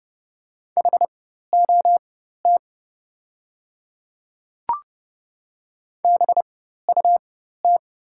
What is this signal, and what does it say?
Morse code, heard as a keyed tone, about 30 wpm, spaced as for 14 wpm, 700 Hz. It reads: HOT BUT
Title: 4x Character Spacing - Top 100 Words - 30wpm